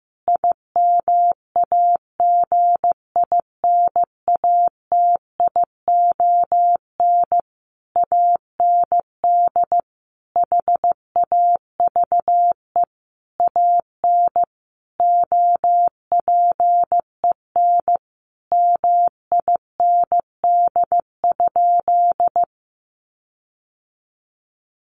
Morse code, heard as a keyed tone, about 15 wpm, 700 Hz. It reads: IMAGINATION AND HAVE AN OPEN MIND?
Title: Sentences from Top 1300 Words 15wpm